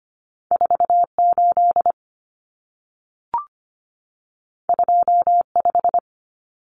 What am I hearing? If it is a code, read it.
Morse code, heard as a keyed tone, about 25 wpm, 700 Hz. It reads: 48 25
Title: Numbers - 2-Digit 25wpm